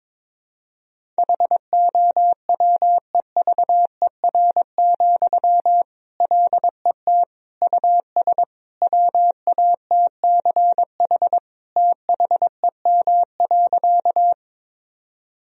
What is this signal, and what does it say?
Morse code, heard as a keyed tone, about 22 wpm, 700 Hz. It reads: HOWEVER, LET US WATCH THEM.
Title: Sentences from Top 1000 Words - Review Part 1 - 22wpm